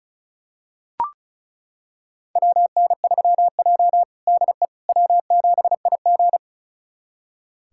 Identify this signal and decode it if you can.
Morse code, heard as a keyed tone, about 36 wpm, 700 Hz. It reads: WN3J DE W7IG
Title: USA Callsign DE Callsign - 4 Characters - 35wpm